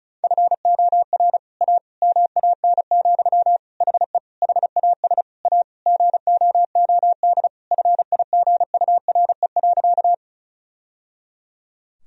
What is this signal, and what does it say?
Morse code, heard as a keyed tone, about 34 wpm, 700 Hz. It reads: FOR A MAN, HE HAS A GOOD FIGURE.
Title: Sentences from Top 1000 Words - Review Part 2 - 35wpm